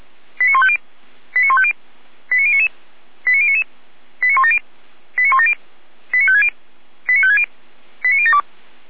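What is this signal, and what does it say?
Five-tone selective call (ZVEI), heard as quick runs of five tones, tones of about 70 ms. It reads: 89150 89150 89099 89099 89170 89170 89560 89560 89072
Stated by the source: Five-tone ZVEI selective call received on SDR.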